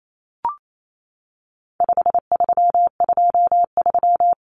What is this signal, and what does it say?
Morse code, heard as a keyed tone, about 28 wpm, 700 Hz. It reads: 5323
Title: Numbers - 4-Digit 28wpm